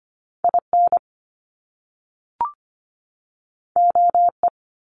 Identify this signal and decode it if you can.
Morse code, heard as a keyed tone, about 25 wpm, 700 Hz. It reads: IN OE
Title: Top Two Letter Combinations in Words 25wpm